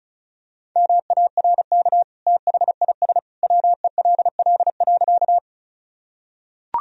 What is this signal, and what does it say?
Morse code, heard as a keyed tone, about 36 wpm, 700 Hz. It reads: MARK THIS WELL.
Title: Sentences from Top 1000 Words - Review Part 1 - 35wpm